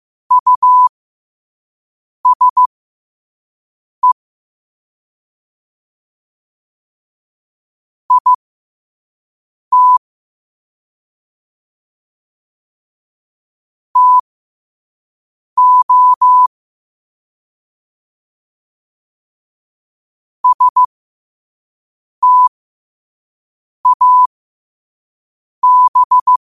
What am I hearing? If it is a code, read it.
Morse code, heard as a keyed tone, about 15 wpm, spaced as for 5 wpm, 1000 Hz. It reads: USE IT TO STAB
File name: qotdfiles / 5wpm